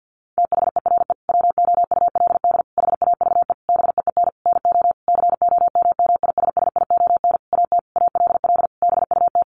Morse code, heard as a keyed tone, about 50 wpm, 700 Hz. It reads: THERE WOULD HAVE BEEN NO COMMISSION AT ALL BUT
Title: Sentences from Top 1400 Words 50wpm